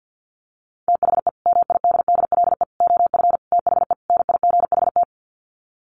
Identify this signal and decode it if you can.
Morse code, heard as a keyed tone, about 50 wpm, 700 Hz. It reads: THE MIDDLE OF THE NIGHT
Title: Sets of 5 Words 50wpm